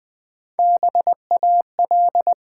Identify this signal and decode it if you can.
Morse code, heard as a keyed tone, about 20 wpm, 700 Hz. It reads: BAL
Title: Build A Word - 3 Letters At A Time - 8-Letter Words - 20wpm